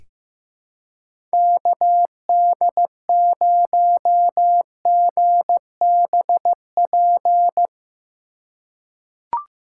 Morse code, heard as a keyed tone, about 16 wpm, 700 Hz. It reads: KD0GBP
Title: USA Call Signs - 6 Characters - 4x Character Spacing - 15wpm